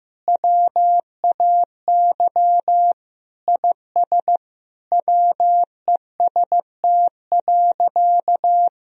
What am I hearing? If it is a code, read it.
Morse code, heard as a keyed tone, about 15 wpm, 700 Hz. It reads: WAY IS WEST.
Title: Sentences from Top 1000 Words - Review Part 3 - 15wpm